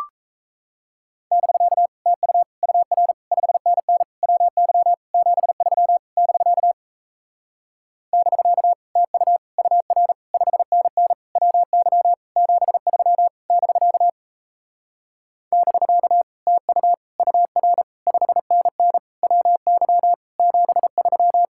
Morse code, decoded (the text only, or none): <BK> TU UR 5NN WY 73 <BK> <BK> TU UR 5NN WY 73 <BK> <BK> TU UR 5NN WY 73